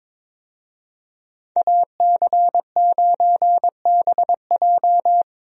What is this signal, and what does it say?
Morse code, heard as a keyed tone, about 22 wpm, 700 Hz. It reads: AC9BJ
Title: USA Call Signs - 5 Characters - 22wpm